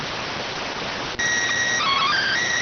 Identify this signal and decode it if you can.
SSTV (slow-scan television), PD 290.